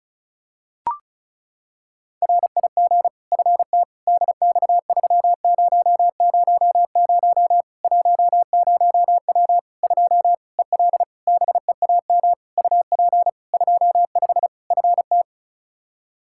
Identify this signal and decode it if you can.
Morse code, heard as a keyed tone, about 36 wpm, 700 Hz. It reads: RIG FT DX3000 10W 2 EL BEAM UP 25 FT
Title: Practice QSO - Exchange 2 - Info - 35wpm